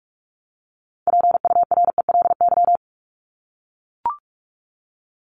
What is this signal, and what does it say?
Morse code, heard as a keyed tone, about 44 wpm, 700 Hz. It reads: PURELY
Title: LY Suffix - Top 100 Words with Base Words 45wpm